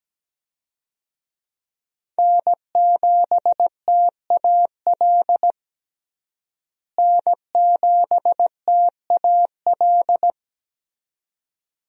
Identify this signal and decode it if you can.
Morse code, heard as a keyed tone, about 17 wpm, 700 Hz. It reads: N7TAL N7TAL
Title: USA Call Signs - 5 Characters - Repeated 2x - 17wpm